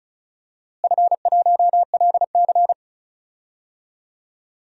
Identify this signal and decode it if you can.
Morse code, heard as a keyed tone, about 36 wpm, 690 Hz. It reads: F1LC
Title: DXCC-100 Call Signs - 4 Characters - Repeated 3x - 35wpm